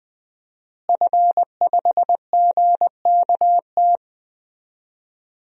Morse code, heard as a keyed tone, about 20 wpm, 700 Hz. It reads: F5GKT
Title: DXCC-100 Call Signs - 5 Characters - Repeated 3x - 20wpm